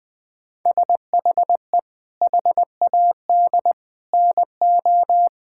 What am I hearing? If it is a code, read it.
Morse code, heard as a keyed tone, about 20 wpm, 700 Hz. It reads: SHE HAD NO
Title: Sentences from Top 1000 Words - Review Part 6 - 20wpm